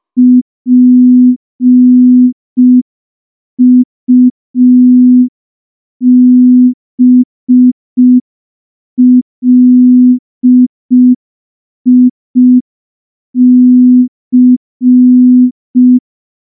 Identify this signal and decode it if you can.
Morse code, heard as a keyed tone, about 5 wpm, 250 Hz. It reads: PUBLIC
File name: morse / morse-training-05WPM